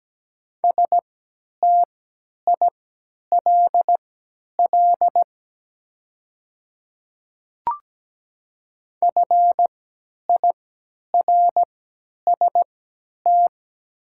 Morse code, heard as a keyed tone, about 17 wpm, spaced as for 10 wpm, 700 Hz. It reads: STILL FIRST
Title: 3x Character Spacing - Top 300 Words - 17wpm